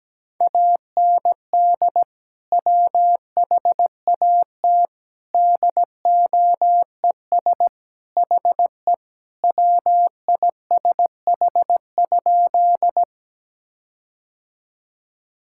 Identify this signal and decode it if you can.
Morse code, heard as a keyed tone, about 17 wpm, 700 Hz. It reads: AND WHAT DOES HE WISH?
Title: Sentences from Top 1600 Words 17wpm